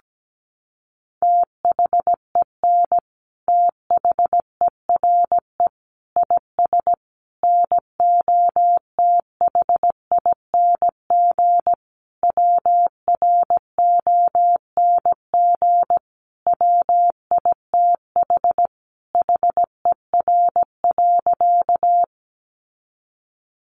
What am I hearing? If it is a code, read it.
Morse code, heard as a keyed tone, about 17 wpm, 700 Hz. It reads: THEN THERE IS NOTHING WRONG WITH HER.
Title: Sentences from Top 900 Words 17wpm